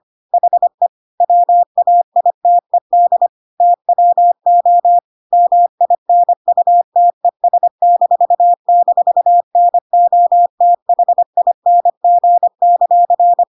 Morse code, heard as a keyed tone, about 25 wpm, 700 Hz. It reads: HE WAITED TWO MINUTES--NOTHING;
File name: Pinocchio / Pinocchio ch1 25 25 ik1hge